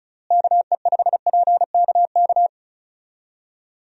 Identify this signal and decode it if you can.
Morse code, heard as a keyed tone, about 36 wpm, 700 Hz. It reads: KE5PKK
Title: USA Call Signs - 6 Characters - Repeated 2x - 35wpm